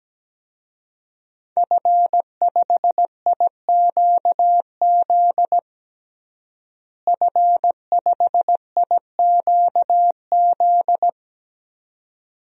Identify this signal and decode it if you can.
Morse code, heard as a keyed tone, about 17 wpm, 700 Hz. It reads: F5IQZ F5IQZ
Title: DXCC-100 Call Signs - 5 Characters - Repeated 2x - 17wpm